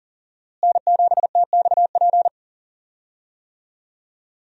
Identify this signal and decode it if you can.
Morse code, heard as a keyed tone, about 40 wpm, 700 Hz. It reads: N7TXP
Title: USA Call Signs - 5 Characters - Repeated 2x - 40wpm